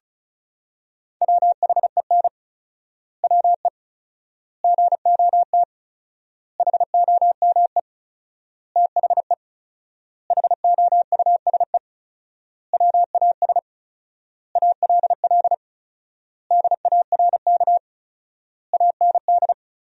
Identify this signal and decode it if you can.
Morse code, heard as a keyed tone, about 36 wpm, 700 Hz. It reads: WHEN WE GOT HOME THE HOUSE WAS ALL DARK AND
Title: 4x Word-Spacing - Sentences from Top 500 Words - 35wpm